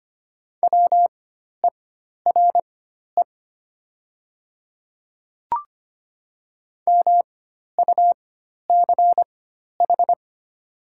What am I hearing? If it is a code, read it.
Morse code, heard as a keyed tone, about 25 wpm, spaced as for 12 wpm, 700 Hz. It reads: WERE MUCH
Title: 4x Character Spacing - Top 200 Words - 25wpm